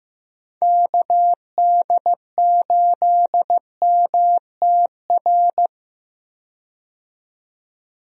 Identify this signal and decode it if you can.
Morse code, heard as a keyed tone, about 15 wpm, 700 Hz. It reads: KD8MTR
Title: USA Call Signs - 6 Characters - 15wpm